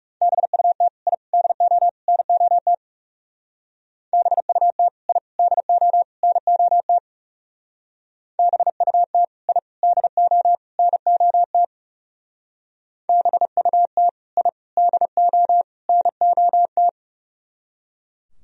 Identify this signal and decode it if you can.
Morse code, heard as a keyed tone, about 36 wpm, 700 Hz. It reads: BUT I DO NOT BUT I DO NOT BUT I DO NOT BUT I DO NOT
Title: Speed-Racing - Sets of 4 Words 30wpm